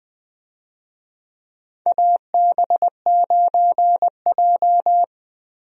Morse code, heard as a keyed tone, about 20 wpm, 700 Hz. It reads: AB9J
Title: USA Call Signs - 4 Characters - Repeated 2x - 20wpm